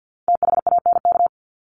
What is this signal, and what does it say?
Morse code, heard as a keyed tone, about 50 wpm, 700 Hz. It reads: THANK
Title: Top 501-750 Words 50wpm